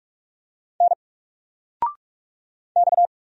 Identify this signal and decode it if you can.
Morse code, heard as a keyed tone, about 44 wpm, 700 Hz. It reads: N X
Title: Single Letter-Number Alt 45wpm